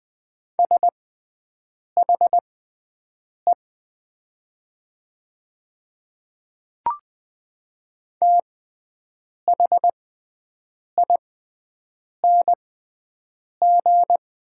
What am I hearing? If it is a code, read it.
Morse code, heard as a keyed tone, about 20 wpm, spaced as for 7 wpm, 700 Hz. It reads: SHE THING
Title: 6x Character Spacing - Top 200 Words - 20wpm